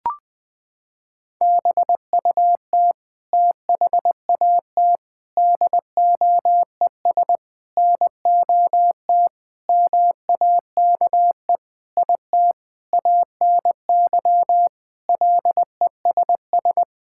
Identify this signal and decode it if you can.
Morse code, heard as a keyed tone, about 20 wpm, 700 Hz. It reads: BUT THAT DOES NOT MAKE IT ANY LESS
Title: Sentences with 7-Word Phrases - 20wpm